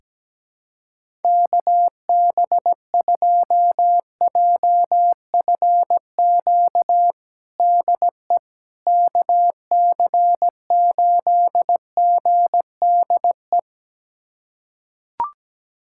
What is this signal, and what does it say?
Morse code, heard as a keyed tone, about 17 wpm, 700 Hz. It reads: KB2JFQ DE KC8GDE
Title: USA Callsign DE Callsign - 6 Characters - 17wpm